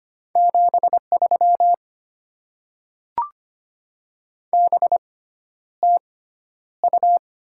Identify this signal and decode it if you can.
Morse code, heard as a keyed tone, about 25 wpm, spaced as for 11 wpm, 690 Hz. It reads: 73 BTU
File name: Top 20 QSO Elements - 6x Character Spacing - 25wpm